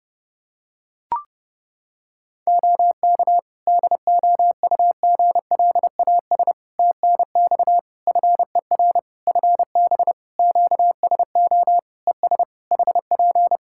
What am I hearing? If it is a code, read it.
Morse code, heard as a keyed tone, about 30 wpm, 700 Hz. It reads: OK DOUGLAS TNX FER FB QSO ES HP
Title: Practice QSO - Exchange 4 - Ending - 30wpm